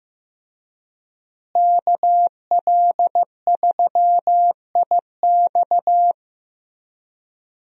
Morse code, heard as a keyed tone, about 15 wpm, 700 Hz. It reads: KL3IX
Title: USA Call Signs - 5 Characters - Repeated 3x - 15wpm